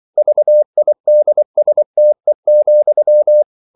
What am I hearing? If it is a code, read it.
Morse code, heard as a keyed tone, about 24 wpm, 600 Hz. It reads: VIDSTE,
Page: Ferie 24 wpm